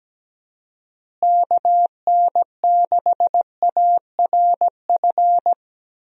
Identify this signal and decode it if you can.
Morse code, heard as a keyed tone, about 17 wpm, 700 Hz. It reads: KN6ARF
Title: USA Call Signs - 6 Characters - Repeated 3x - 17wpm